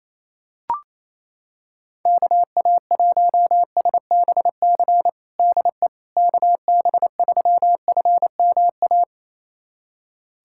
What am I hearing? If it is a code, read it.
Morse code, heard as a keyed tone, about 28 wpm, 700 Hz. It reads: KA1SBC DE KB3FMA
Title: USA Callsign DE Callsign - 6 Characters - 28wpm